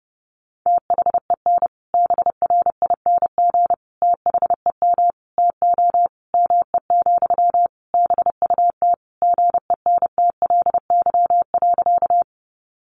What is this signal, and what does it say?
Morse code, heard as a keyed tone, about 30 wpm, 700 Hz. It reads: THEN BRING THEM TO ME, BUT GENTLY.
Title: Sentences from Top 1500 Words 30wpm